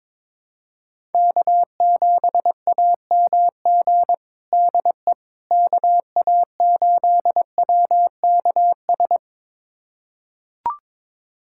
Morse code, heard as a keyed tone, about 22 wpm, 700 Hz. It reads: K7AMG DE KA8WKS
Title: USA Callsign DE Callsign 22wpm